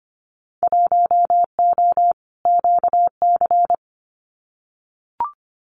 Morse code, heard as a keyed tone, about 25 wpm, 700 Hz. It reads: 1O QC
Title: Winter Field Day Exchanges 25wpm